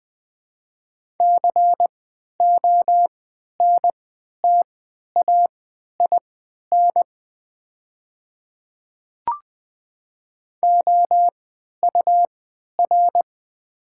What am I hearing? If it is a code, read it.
Morse code, heard as a keyed tone, about 20 wpm, spaced as for 11 wpm, 700 Hz. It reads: CONTAIN OUR
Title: 3x Character Spacing - Top 500 Words - 20wpm